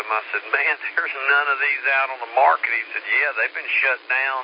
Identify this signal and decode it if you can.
Emergency Alert System (SAME) tones, heard as harsh buzzing bursts, with speech. It is heard from 0.3 to 3.6 s.